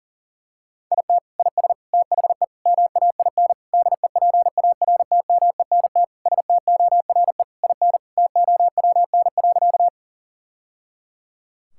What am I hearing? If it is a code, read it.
Morse code, heard as a keyed tone, about 37 wpm, 700 Hz. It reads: IT IS THE MAIN DEPARTMENT STORE IN TOWN.
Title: Sentences from Top 1000 Words - Review Part 5 - 40wpm